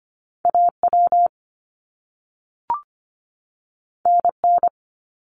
Morse code, heard as a keyed tone, about 25 wpm, 700 Hz. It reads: AW NN
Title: Top Two Letter Combinations in Words 25wpm